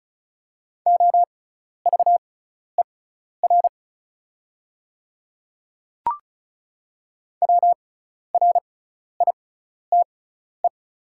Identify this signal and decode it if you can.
Morse code, heard as a keyed tone, about 36 wpm, spaced as for 12 wpm, 700 Hz. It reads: OVER WRITE
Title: 6x Character Spacing - Top 100 Words - 35wpm